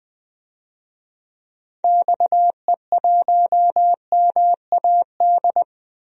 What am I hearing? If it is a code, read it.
Morse code, heard as a keyed tone, about 20 wpm, 700 Hz. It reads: XE1MAD
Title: DXCC-100 Call Signs - 6 Characters - 20wpm